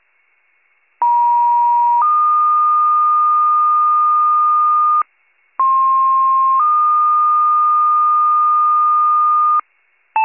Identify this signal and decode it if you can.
Two-tone page, heard as a one-second tone, then a three-second tone, about 950 then 1220 Hz, about 1060 then 1220 Hz. It is heard from 1.0 to 9.6 s.